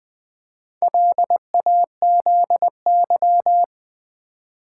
Morse code, heard as a keyed tone, about 20 wpm, 700 Hz. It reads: LAZY
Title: Build A Word - 2 Letters At A Time - 4-Letter Words - 20wpm